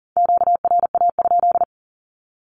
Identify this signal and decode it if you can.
Morse code, heard as a keyed tone, about 40 wpm, 700 Hz. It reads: QRA?
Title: Q-Codes 40wpm